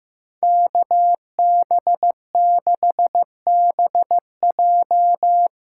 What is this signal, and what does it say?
Morse code, heard as a keyed tone, about 15 wpm, 700 Hz. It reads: KB6BJ
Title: USA Call Signs - 5 Characters - 2x Character Spacing - 15wpm